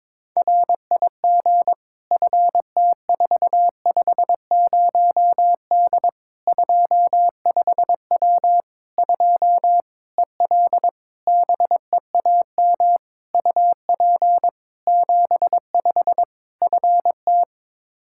Morse code, heard as a keyed tone, about 22 wpm, 700 Hz. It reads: RIG FT450D 25W 2 EL BEAM UP 75 FT
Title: Practice QSO - Exchange 2 - Info - 22wpm